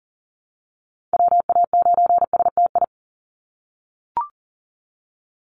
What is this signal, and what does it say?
Morse code, heard as a keyed tone, about 40 wpm, 700 Hz. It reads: WA9STI
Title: NA SOTA Chaser Callsigns 40wpm